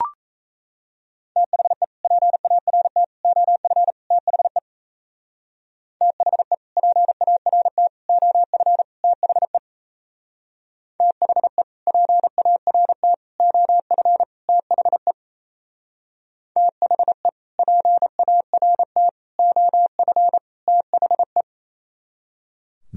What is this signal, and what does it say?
Morse code, heard as a keyed tone, about 33 wpm, 700 Hz. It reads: THE PART OF THE THE PART OF THE THE PART OF THE THE PART OF THE
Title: Speed-Racing - Sets of 4 Words 28wpm